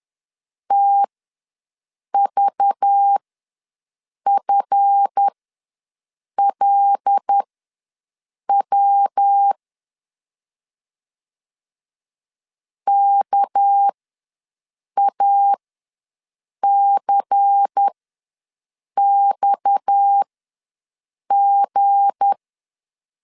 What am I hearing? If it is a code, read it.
Morse code, heard as a keyed tone, about 10 wpm, spaced as for 6 wpm, 800 Hz. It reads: TVFLW KACXG
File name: LEZIONI / 7 GF9Z / 7 GF ws 10wpm 3918